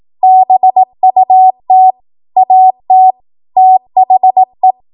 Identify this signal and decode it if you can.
Morse code, heard as a keyed tone, about 18 wpm, 750 Hz. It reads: BUT AT THE
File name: Morse / 18WPM